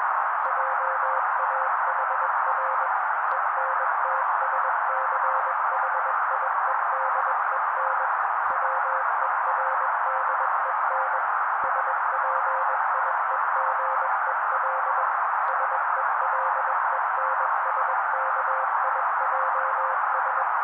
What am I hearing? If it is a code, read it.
Morse code, heard as a keyed tone, about 21 wpm, 530 Hz. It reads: JAHR ENTSCHIEDEN WERDEN SPIEGEL SELENSKIJS